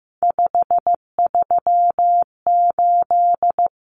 Morse code, heard as a keyed tone, about 15 wpm, 700 Hz. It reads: 538
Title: Common Cut Numbers - 3-Digit 15wpm